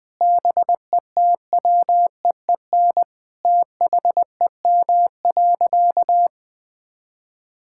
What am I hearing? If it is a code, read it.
Morse code, heard as a keyed tone, about 20 wpm, 700 Hz. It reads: BETWEEN THEM.
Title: Sentences from Top 1000 Words - Review Part 6 - 20wpm